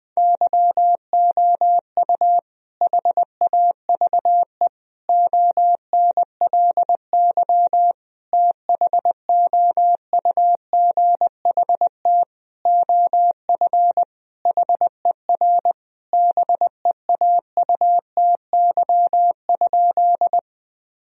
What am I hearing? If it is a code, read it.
Morse code, heard as a keyed tone, about 20 wpm, 700 Hz. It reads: YOU HAVE ONLY THOUGHT OF HER BEAUTY?
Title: Sentences from Top 1000 Words - Review Part 2 - 20wpm